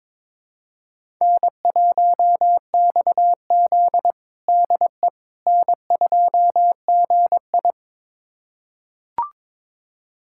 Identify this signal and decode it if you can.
Morse code, heard as a keyed tone, about 22 wpm, 700 Hz. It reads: N1XZ DE N2GI
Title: USA Callsign DE Callsign - 4 Characters - 22wpm